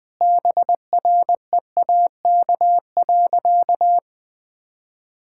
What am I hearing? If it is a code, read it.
Morse code, heard as a keyed tone, about 20 wpm, 700 Hz. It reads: BREAK.
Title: Sentences from Top 1000 Words - Review Part 6 - 20wpm